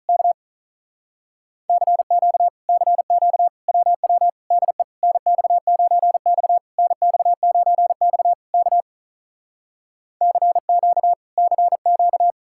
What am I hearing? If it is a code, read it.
Morse code, heard as a keyed tone, about 47 wpm, 700 Hz. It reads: K CQ CQ WW DE NX9X NX9X K CQ CQ